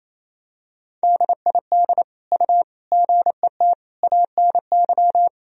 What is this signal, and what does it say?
Morse code, heard as a keyed tone, about 28 wpm, 700 Hz. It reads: DID U GET ANY
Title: Sentences from Top 400 Words and Abbreviations 28wpm